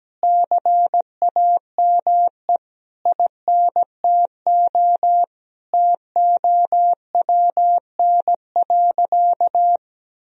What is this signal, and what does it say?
Morse code, heard as a keyed tone, about 17 wpm, 700 Hz. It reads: CAME INTO TOWN.
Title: Sentences from Top 1000 Words - Review Part 2 - 17wpm